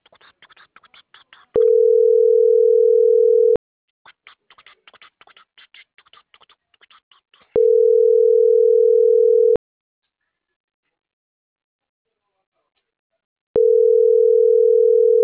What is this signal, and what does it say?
Ringback tone (North America), 440 + 480 Hz.